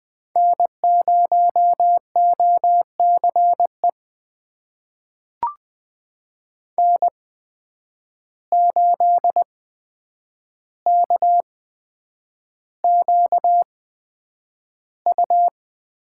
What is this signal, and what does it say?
Morse code, heard as a keyed tone, about 20 wpm, spaced as for 9 wpm, 700 Hz. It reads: N0OCE N8KQU
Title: USA Call Signs - 8x Character Spacing - 20wpm